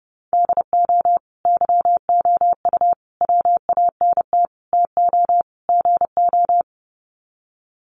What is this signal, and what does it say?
Morse code, heard as a keyed tone, about 30 wpm, 700 Hz. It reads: DO YOU WANT TO GO
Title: Sets of 5 Words Encore2 30wpm